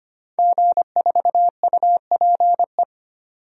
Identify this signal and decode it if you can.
Morse code, heard as a keyed tone, about 25 wpm, 700 Hz. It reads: G4UPE
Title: DXCC-100 Call Signs - 5 Characters - 25wpm